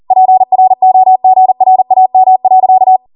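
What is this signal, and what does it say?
Morse code, heard as a keyed tone, about 40 wpm, 750 Hz. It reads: PROGRAM.
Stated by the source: Code practice at 40 wpm.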